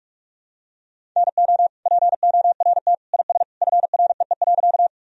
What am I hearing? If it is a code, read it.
Morse code, heard as a keyed tone, about 41 wpm, 700 Hz. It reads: NO PORT IS FREE.
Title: Sentences from Top 1000 Words - Review Part 1 - 45wpm